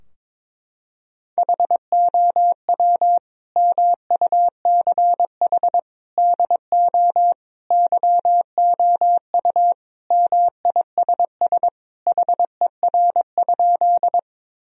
Morse code, heard as a keyed tone, about 22 wpm, 700 Hz. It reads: HOW MUCH DO YOU MISS HER?
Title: Sentences from Top 1000 Words - Review Part 3 - 22wpm